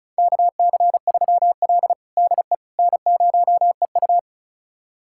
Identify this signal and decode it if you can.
Morse code, heard as a keyed tone, about 36 wpm, 700 Hz. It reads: KC3L DE N0EU
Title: USA Callsign DE Callsign - 4 Characters - 35wpm